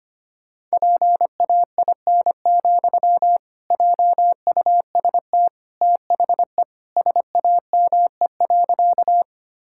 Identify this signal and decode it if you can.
Morse code, heard as a keyed tone, about 25 wpm, 700 Hz. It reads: PAIN, JUST THE SAME.
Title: Sentences from Top 1600 Words 25wpm